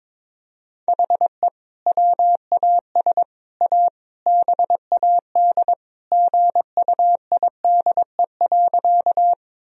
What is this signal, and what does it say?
Morse code, heard as a keyed tone, about 22 wpm, 700 Hz. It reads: HE WAS A BAD GUIDE.
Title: Sentences from Top 1000 Words - Review Part 3 - 22wpm